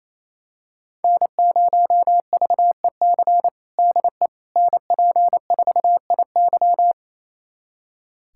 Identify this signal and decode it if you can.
Morse code, heard as a keyed tone, about 28 wpm, 700 Hz. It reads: N0VEC DE NP4IY
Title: USA Callsign DE Callsign - 5 Characters - 28wpm